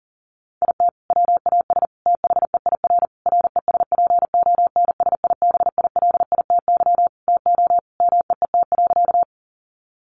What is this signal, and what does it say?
Morse code, heard as a keyed tone, about 40 wpm, 700 Hz. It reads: IT WAS THEIR RESPONSIBILITY TO MEET.
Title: Sentences from Top 1200 Words 40wpm